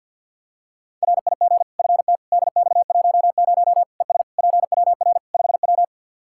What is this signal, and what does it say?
Morse code, heard as a keyed tone, about 52 wpm, 700 Hz. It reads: RIG FT DX10 ES PWR 5W
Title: Practice QSO - Exchange 2 - Info - 50wpm